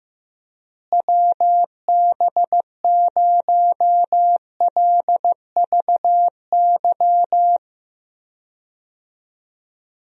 Morse code, heard as a keyed tone, about 15 wpm, 700 Hz. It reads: WB0LVY
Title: USA Call Signs - 6 Characters - Repeated 2x - 15wpm